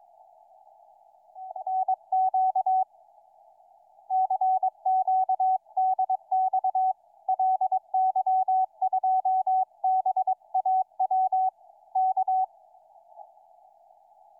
Morse code, heard as a keyed tone, about 22 wpm, 750 Hz. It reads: NQ CQDX LY2BAW K
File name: ly2baw 28-11-13 2003 160m